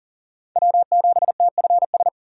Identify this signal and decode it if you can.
Morse code, heard as a keyed tone, about 40 wpm, 700 Hz. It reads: W7TFS
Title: USA Call Signs - 5 Characters - 3x Character Spacing - 40wpm